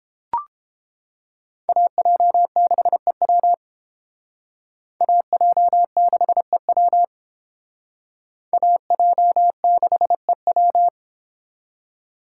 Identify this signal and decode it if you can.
Morse code, heard as a keyed tone, about 31 wpm, 700 Hz. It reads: AJ6EW AJ6EW AJ6EW